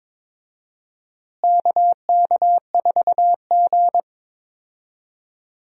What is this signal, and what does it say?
Morse code, heard as a keyed tone, about 22 wpm, 700 Hz. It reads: KK4G
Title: USA Call Signs - 4 Characters - Repeated 3x - 22wpm